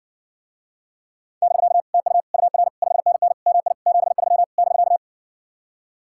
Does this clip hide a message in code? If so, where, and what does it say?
Yes: Morse code from 1.4 to 5.0 s, heard as a keyed tone, about 59 wpm, 700 Hz. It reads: <BK> TU UR 5NN MI 73 <BK>